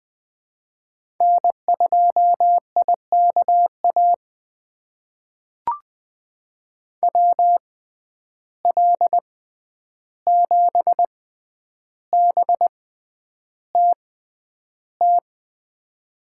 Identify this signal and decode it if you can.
Morse code, heard as a keyed tone, about 20 wpm, spaced as for 10 wpm, 700 Hz. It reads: N2IKA WL7BTT
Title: USA Call Signs - 6x Character Spacing - 20wpm